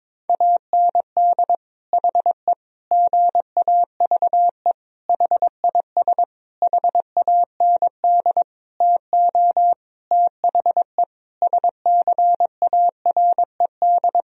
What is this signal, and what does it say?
Morse code, heard as a keyed tone, about 22 wpm, 700 Hz. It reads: AND HE GAVE HIS HAND TO THE SCARED
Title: Sentences from Top 1000 Words - Review Part 4 - 22wpm